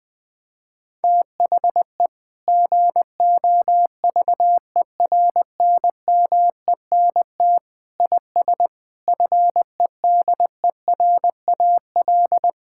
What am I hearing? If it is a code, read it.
Morse code, heard as a keyed tone, about 20 wpm, 700 Hz. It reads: THE GOVERNMENT IS FEDERAL
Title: Sentences from Top 1000 Words - Review Part 5 - 20wpm